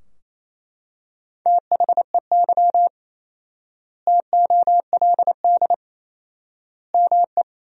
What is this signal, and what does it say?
Morse code, heard as a keyed tone, about 28 wpm, 700 Hz. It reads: THEY TOLD ME
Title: 4x Word-Spacing - Sentences from Top 400 Words - 28wpm